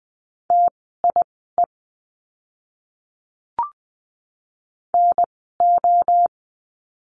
Morse code, heard as a keyed tone, about 20 wpm, spaced as for 15 wpm, 700 Hz. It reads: TIE NO
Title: 054 Lesson 10f - Letter r - Words 20wpm